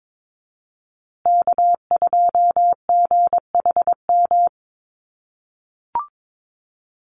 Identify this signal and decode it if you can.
Morse code, heard as a keyed tone, about 22 wpm, 700 Hz. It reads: K2GHM
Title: USA Call Signs Repeated 3x 22wpm